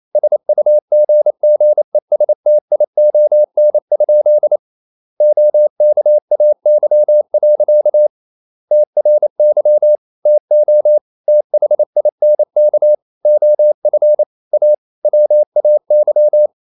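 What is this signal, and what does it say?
Morse code, heard as a keyed tone, about 28 wpm, 600 Hz. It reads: SUGGESTION? OKAY. TRY TO THINK OF A WAY
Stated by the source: Morse code practice at 28 wpm, text of Ken Follett's novel Never in English, from CW med Gnister.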